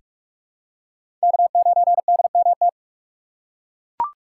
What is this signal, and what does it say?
Morse code, heard as a keyed tone, about 42 wpm, 700 Hz. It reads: K9DMT
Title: USA Call Signs - 5 Characters - 4x Character Spacing - 45wpm